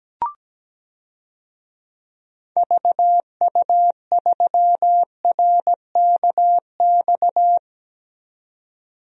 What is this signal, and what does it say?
Morse code, heard as a keyed tone, about 17 wpm, 700 Hz. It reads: VU3RKX
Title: DXCC-100 Call Signs - 6 Characters - Repeated 3x - 17wpm